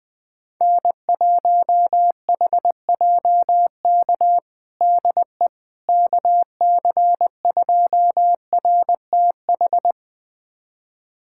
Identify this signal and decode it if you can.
Morse code, heard as a keyed tone, about 20 wpm, 700 Hz. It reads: N1HJK DE KC2RTH
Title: USA Callsign DE Callsign 20wpm